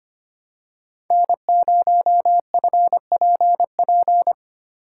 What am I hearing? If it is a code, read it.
Morse code, heard as a keyed tone, about 25 wpm, 700 Hz. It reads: N0FPP
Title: USA Call Signs - 5 Characters - 6x Character Spacing - 25wpm